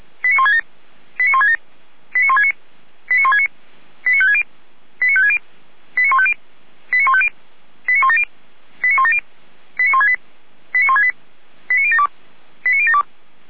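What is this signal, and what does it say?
Five-tone selective call (ZVEI), heard as quick runs of five tones, tones of about 70 ms. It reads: 89167 89167 89169 89169 89560 89560 89150 89150 89170 89170 89168 89168 89072 89072
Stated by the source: ZVEI five-tone call 89220 received by SDR, archived by Feuerwehr Orlamünde fire brigade.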